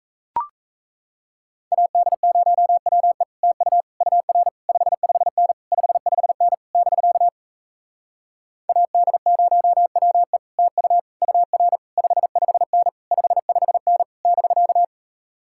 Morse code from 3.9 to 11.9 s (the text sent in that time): UR 55N 55N <BK> AD0WE TU UR